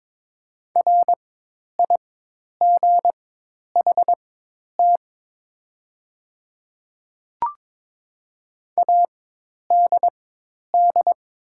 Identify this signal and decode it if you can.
Morse code, heard as a keyed tone, about 22 wpm, spaced as for 10 wpm, 700 Hz. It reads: RIGHT ADD
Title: 4x Character Spacing - Top 200 Words - 22wpm